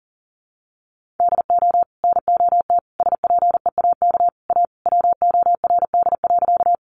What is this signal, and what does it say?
Morse code, heard as a keyed tone, about 40 wpm, 700 Hz. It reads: DO NOT SPEAK A WORD.
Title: Sentences from Top 1100 Words 40wpm